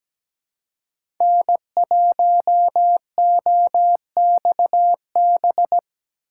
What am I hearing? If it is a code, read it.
Morse code, heard as a keyed tone, about 17 wpm, 700 Hz. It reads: N1OXB
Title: USA Call Signs - 5 Characters - 2x Character Spacing - 17wpm